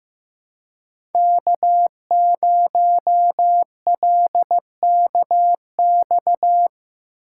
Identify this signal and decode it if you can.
Morse code, heard as a keyed tone, about 15 wpm, 700 Hz. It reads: K0LKX
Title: USA Call Signs - 5 Characters - 6x Character Spacing - 15wpm